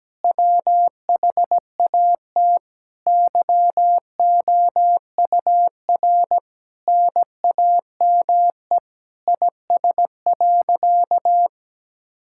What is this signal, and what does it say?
Morse code, heard as a keyed tone, about 17 wpm, 700 Hz. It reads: WHAT YOUR NAME IS.
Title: Sentences from Top 1000 Words - Review Part 6 - 17wpm